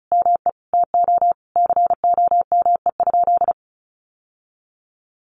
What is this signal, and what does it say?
Morse code, heard as a keyed tone, about 34 wpm, 700 Hz. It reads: ME TO COME?
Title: Sentences from Top 1400 Words 35wpm